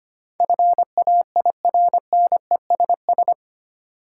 Morse code, heard as a keyed tone, about 25 wpm, 700 Hz. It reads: FAIRNESS
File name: 8 Letter Words - First 3 Letters Given - 25wpm